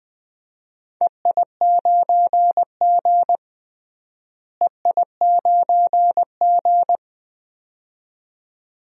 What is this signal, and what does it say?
Morse code, heard as a keyed tone, about 20 wpm, 700 Hz. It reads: EI9G EI9G
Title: DXCC-100 Call Signs - 4 Characters - Repeated 2x - 20wpm